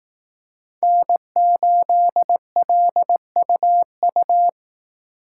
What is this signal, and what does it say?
Morse code, heard as a keyed tone, about 18 wpm, 700 Hz. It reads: N8LUU
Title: Speed Racing - USA 1x3 Call Signs 15wpm